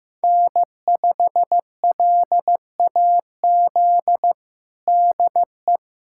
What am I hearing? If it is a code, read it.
Morse code, heard as a keyed tone, about 15 wpm, 700 Hz. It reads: N5LAZ DE
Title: USA Callsign DE Callsign 15wpm